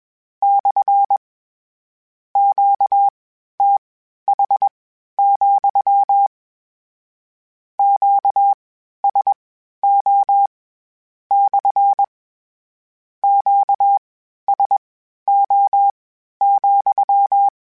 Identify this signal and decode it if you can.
Morse code, heard as a keyed tone, about 21 wpm, spaced as for 12 wpm, 800 Hz. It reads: / QTH, QSO / QSO,
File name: Q Calls / Q Calls-mp3s / Q CALLS SPSL @20